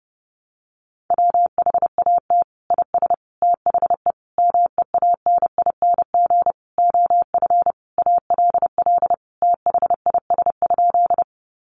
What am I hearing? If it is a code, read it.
Morse code, heard as a keyed tone, about 30 wpm, 700 Hz. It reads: WHAT IS THE MEANING OF ALL THIS?
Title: Sentences from Top 900 Words 30wpm